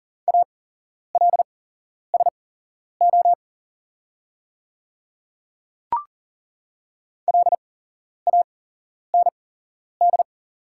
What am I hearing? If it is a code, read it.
Morse code, heard as a keyed tone, about 40 wpm, spaced as for 11 wpm, 690 Hz. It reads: ALSO LAND
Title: 8x Character Spacing - Top 500 Words - 40wpm